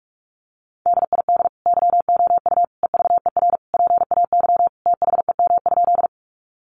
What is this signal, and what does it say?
Morse code, heard as a keyed tone, about 44 wpm, 700 Hz. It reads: DID YOU EVER PAY THEM?
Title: Sentences from Top 700 Words 45wpm